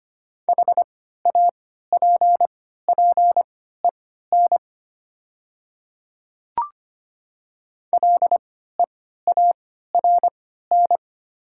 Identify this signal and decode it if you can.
Morse code, heard as a keyed tone, about 25 wpm, spaced as for 14 wpm, 700 Hz. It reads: HAPPEN LEARN
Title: 3x Character Spacing - Top 500 Words - 25wpm